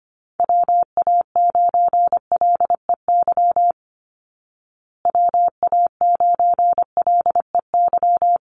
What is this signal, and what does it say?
Morse code, heard as a keyed tone, about 25 wpm, 700 Hz. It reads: WA9LEY WA9LEY
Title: Call Signs Repeated 2x - 25wpm